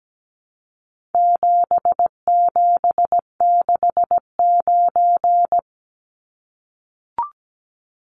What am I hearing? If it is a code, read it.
Morse code, heard as a keyed tone, about 17 wpm, 700 Hz. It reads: 7769
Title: Numbers - 4-Digit 17wpm